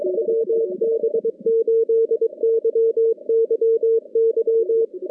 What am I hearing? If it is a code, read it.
Morse code, heard as a keyed tone, about 22 wpm, 440 Hz. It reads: WB8YYY
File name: contests / CW-SS-k5tr- -2100